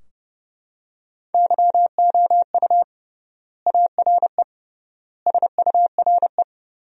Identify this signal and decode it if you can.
Morse code, heard as a keyed tone, about 32 wpm, 700 Hz. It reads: YOU ARE SURE
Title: 3x Word-Spacing - Sentences from Top 300 Words - 30wpm